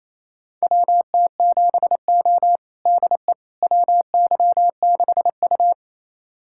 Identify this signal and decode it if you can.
Morse code, heard as a keyed tone, about 28 wpm, 700 Hz. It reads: WT7O DE WY6U
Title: USA Callsign DE Callsign - 4 Characters - 28wpm